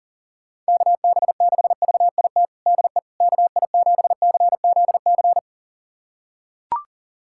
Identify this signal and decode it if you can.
Morse code, heard as a keyed tone, about 40 wpm, 700 Hz. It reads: KB6VIT DE KI7CZC
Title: USA Callsign DE Callsign - 6 Characters - 40wpm